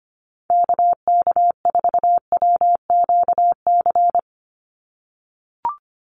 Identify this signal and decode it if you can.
Morse code, heard as a keyed tone, about 25 wpm, 700 Hz. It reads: KK4WQC
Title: USA Call Signs 25wpm